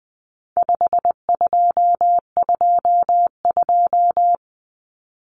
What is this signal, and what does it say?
Morse code, heard as a keyed tone, about 19 wpm, 700 Hz. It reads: 5222
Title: Numbers - 4-Digit 20wpm